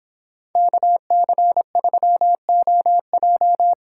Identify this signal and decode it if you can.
Morse code, heard as a keyed tone, about 24 wpm, 700 Hz. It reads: KC3OJ